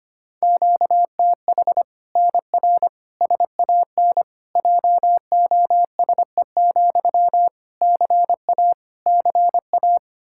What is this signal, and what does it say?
Morse code, heard as a keyed tone, about 25 wpm, 700 Hz. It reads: QTH NR SAN JOSE, CA CA
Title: Practice QSO - Exchange 1 - Beginning - 25wpm